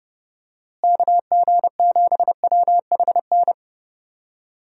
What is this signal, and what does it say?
Morse code, heard as a keyed tone, about 30 wpm, 700 Hz. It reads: KG7WHN
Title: USA Call Signs - 6 Characters - Repeated 2x - 30wpm